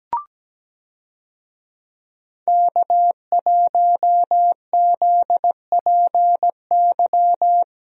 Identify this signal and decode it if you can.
Morse code, heard as a keyed tone, about 17 wpm, 700 Hz. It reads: K1ZPY
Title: USA Call Signs - 5 Characters - Repeated 3x - 17wpm